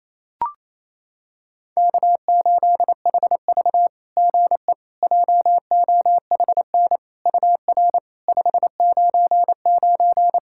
Morse code, heard as a keyed tone, about 27 wpm, 690 Hz. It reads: K8HV GE JOHN UR 599
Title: SOTA QSO Practice 28wpm